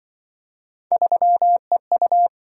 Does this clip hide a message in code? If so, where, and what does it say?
Yes: Morse code from 0.9 to 2.3 s, heard as a keyed tone, about 27 wpm, 700 Hz. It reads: 3EU